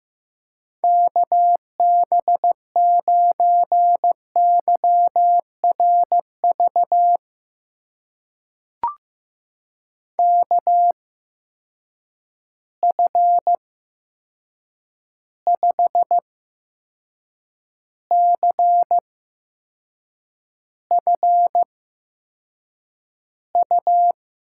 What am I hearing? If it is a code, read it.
Morse code, heard as a keyed tone, about 15 wpm, spaced as for 6 wpm, 700 Hz. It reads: KB9YRV KF5CFU
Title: USA Call Signs - 6 Characters - 8x Character Spacing - 15wpm